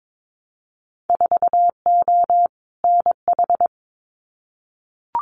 Morse code, heard as a keyed tone, about 22 wpm, 700 Hz. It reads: 4O NH
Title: Winter Field Day Exchanges Encore 22wpm